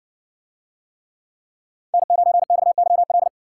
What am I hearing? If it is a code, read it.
Morse code, heard as a keyed tone, about 60 wpm, 700 Hz. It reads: N9XYL